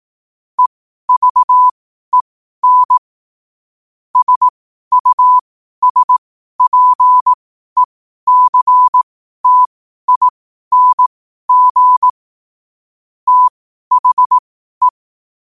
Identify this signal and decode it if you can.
Morse code, heard as a keyed tone, about 18 wpm, spaced as for 12 wpm, 1000 Hz. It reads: EVEN SUSPECTING THE
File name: qotdfiles / 12wpm